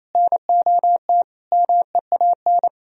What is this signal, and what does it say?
Morse code, heard as a keyed tone, about 33 wpm, 700 Hz. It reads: NOT MEAN
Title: Speed-Racing - Sets of 4 Words 28wpm